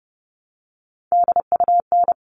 Morse code, heard as a keyed tone, about 30 wpm, 700 Hz. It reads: DUN
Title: All Cut Numbers - 3-Digit 30wpm